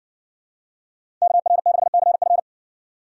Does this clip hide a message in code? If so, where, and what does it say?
Yes: Morse code from 1.2 to 2.4 s, heard as a keyed tone, about 56 wpm, 700 Hz. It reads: KR6CF